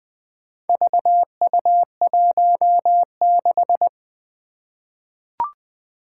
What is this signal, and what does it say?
Morse code, heard as a keyed tone, about 20 wpm, 700 Hz. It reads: VU16
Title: Four Challenging Letter-Number Combinations 20wpm